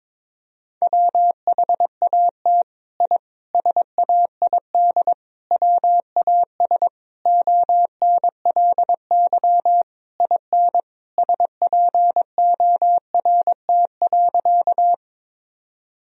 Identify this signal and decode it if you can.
Morse code, heard as a keyed tone, about 22 wpm, 700 Hz. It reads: WHAT I SAID WAS ONLY IN SPORT.
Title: Sentences from Top 1000 Words - Review Part 4 - 22wpm